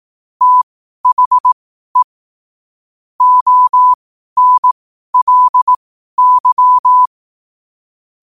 Morse code, heard as a keyed tone, about 18 wpm, spaced as for 12 wpm, 1000 Hz. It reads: THE ONLY
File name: 2026-01-07-12wpm